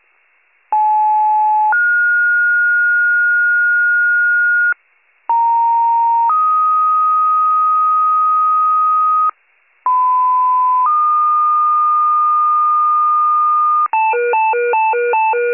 Two-tone page, heard as a one-second tone, then a three-second tone, about 860 then 1430 Hz, about 930 then 1250 Hz, about 1010 then 1220 Hz. It is heard from 0.7 to 13.9 s.